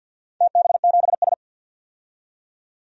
Morse code, heard as a keyed tone, about 50 wpm, 700 Hz. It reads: TB7S
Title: DXCC-100 Call Signs - 4 Characters - 50wpm